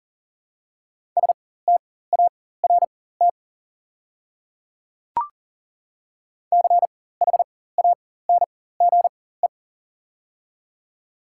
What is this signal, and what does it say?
Morse code, heard as a keyed tone, about 40 wpm, spaced as for 19 wpm, 700 Hz. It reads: START CHANGE
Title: 4x Character Spacing - Top 300 Words - 40wpm